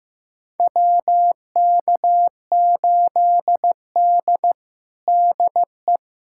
Morse code, heard as a keyed tone, about 15 wpm, 700 Hz. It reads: WK8D DE
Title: USA Callsign DE Callsign - 4 Characters - 15wpm